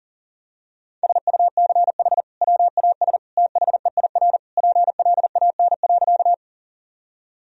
Morse code, heard as a keyed tone, about 40 wpm, 700 Hz. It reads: SUCH WAS THEIR PLAN.
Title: Sentences from Top 1000 Words - Review Part 2 - 40wpm